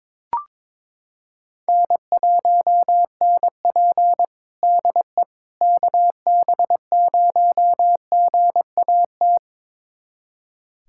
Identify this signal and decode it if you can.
Morse code, heard as a keyed tone, about 22 wpm, 700 Hz. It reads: N1NP DE KB0GAT
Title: USA Callsign DE Callsign 22wpm